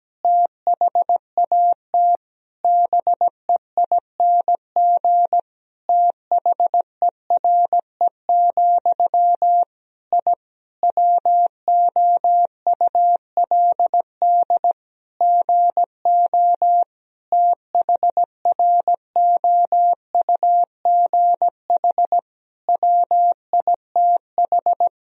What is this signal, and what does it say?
Morse code, heard as a keyed tone, about 17 wpm, 700 Hz. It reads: THAT BEING THERE, I WOULD GO THROUGH WITH
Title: Sentences from Top 1000 Words - Review Part 3 - 17wpm